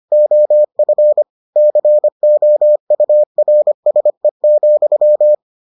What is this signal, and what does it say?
Morse code, heard as a keyed tone, about 25 wpm, 600 Hz. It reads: OF COURSE,